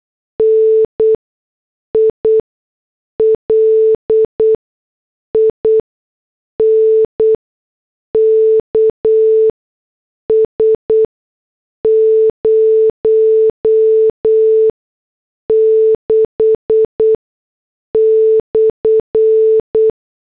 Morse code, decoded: NI LINKS 06/